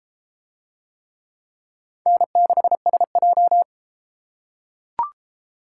Morse code, heard as a keyed tone, about 31 wpm, 700 Hz. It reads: N6SJ